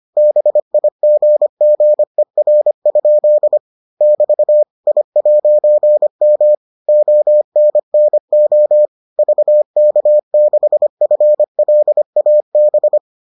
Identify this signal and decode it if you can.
Morse code, heard as a keyed tone, about 25 wpm, 600 Hz. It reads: BIGGER? <BT> I'M ONNO VK6FLAB